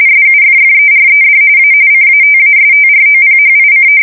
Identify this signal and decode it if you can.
RTTY, about 45.45 baud. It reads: SUFFERED A CATASTROPHIC
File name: fsk170 / battleshipnj